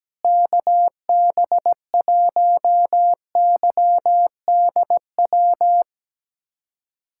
Morse code, heard as a keyed tone, about 17 wpm, 700 Hz. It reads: KB1YDW
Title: USA Callsign DE Callsign - 6 Characters - 17wpm